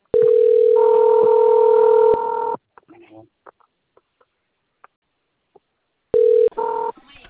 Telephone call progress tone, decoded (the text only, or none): ringback tone